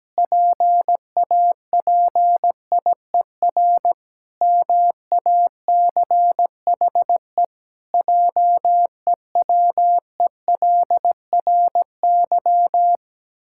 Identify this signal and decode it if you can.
Morse code, heard as a keyed tone, about 17 wpm, 700 Hz. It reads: PAPIER MACHE JEWELRY
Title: Pangram Challenge v2 - 17wpm